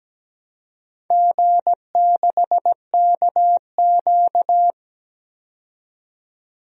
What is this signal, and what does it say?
Morse code, heard as a keyed tone, about 17 wpm, 700 Hz. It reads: G6KQ
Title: DXCC-100 Call Signs - 4 Characters - 17wpm